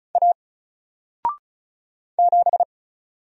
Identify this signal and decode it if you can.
Morse code, heard as a keyed tone, about 34 wpm, 700 Hz. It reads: A 7
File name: Single-Letter-Number-35wpm